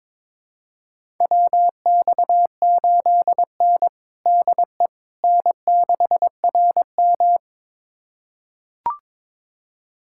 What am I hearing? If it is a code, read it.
Morse code, heard as a keyed tone, about 22 wpm, 700 Hz. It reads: WX8N DE N6RM
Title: USA Callsign DE Callsign - 4 Characters - 22wpm